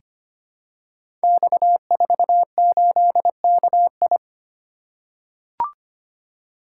Morse code, heard as a keyed tone, about 25 wpm, 700 Hz. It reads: X48KI
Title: Code Groups - 5 Letter:Numbers - 25wpm